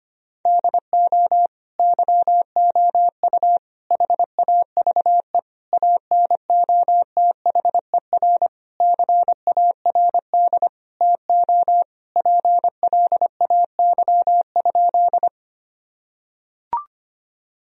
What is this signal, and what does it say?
Morse code, heard as a keyed tone, about 25 wpm, 700 Hz. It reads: DO YOU HAVE ANOTHER CARD TO PLAY?
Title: Sentences from Top 1000 Words - Review Part 4 - 25wpm